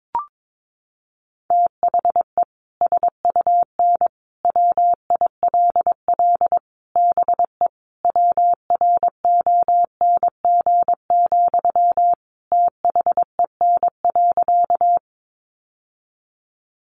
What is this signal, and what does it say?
Morse code, heard as a keyed tone, about 22 wpm, 700 Hz. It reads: THE SUN WILL BE WRONG, THEN.
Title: Sentences from Top 900 Words 22wpm